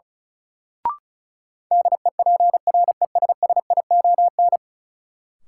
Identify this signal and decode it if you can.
Morse code, heard as a keyed tone, about 34 wpm, 700 Hz. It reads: DEPRESSION
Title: Sentences from Top 1300 Words - New Word Drill - 35wpm